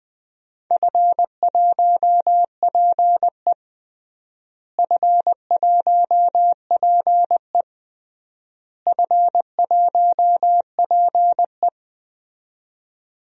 Morse code, heard as a keyed tone, about 20 wpm, 700 Hz. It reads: F1PE F1PE F1PE
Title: DXCC-100 Call Signs - 4 Characters - Repeated 3x - 20wpm